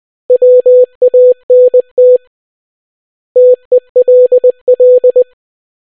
Morse code, heard as a keyed tone, about 20 wpm, 500 Hz. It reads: WANT TELL